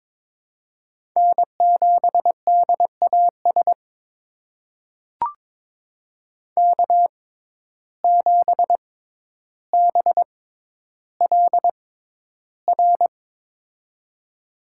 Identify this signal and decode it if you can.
Morse code, heard as a keyed tone, about 22 wpm, spaced as for 11 wpm, 700 Hz. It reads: N7DAS K7BLR
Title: USA Call Signs - 5 Characters - 6x Character Spacing - 22wpm